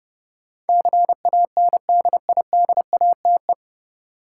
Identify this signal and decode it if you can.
Morse code, heard as a keyed tone, about 30 wpm, 700 Hz. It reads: CANDIDATE
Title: Sentences from Top 1600 Words - New Word Drill - 30wpm